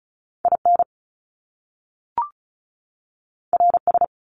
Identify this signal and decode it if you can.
Morse code, heard as a keyed tone, about 36 wpm, 700 Hz. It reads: IN RS
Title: Top Two Letter Combinations in Words 35wpm